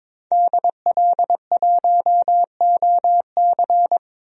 Morse code, heard as a keyed tone, about 21 wpm, 700 Hz. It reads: DL1OC
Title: DXCC-100 Call Signs - 5 Characters - 22wpm